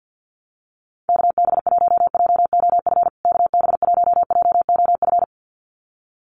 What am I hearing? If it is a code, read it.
Morse code, heard as a keyed tone, about 48 wpm, 700 Hz. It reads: KB1JOF KB1JOF
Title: USA Call Signs Repeated 2x 50wpm